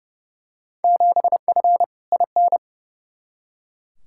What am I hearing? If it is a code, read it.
Morse code, heard as a keyed tone, about 30 wpm, 700 Hz. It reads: 7F IN
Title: Field Day Exchanges Encore - 30wpm